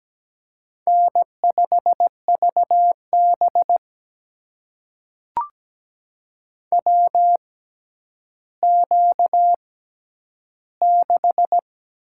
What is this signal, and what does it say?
Morse code, heard as a keyed tone, about 17 wpm, spaced as for 10 wpm, 700 Hz. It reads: N5VB WQ6
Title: USA Call Signs - 4 Characters - 6x Character Spacing - 17wpm